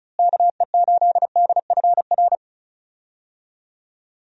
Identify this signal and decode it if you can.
Morse code, heard as a keyed tone, about 34 wpm, 700 Hz. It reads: KE8DFR
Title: USA Call Signs - 6 Characters - Repeated 3x - 35wpm